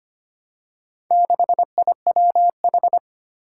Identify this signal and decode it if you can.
Morse code, heard as a keyed tone, about 25 wpm, 700 Hz. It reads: 6IWH
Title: Four Challenging Letter-Number Combinations 25wpm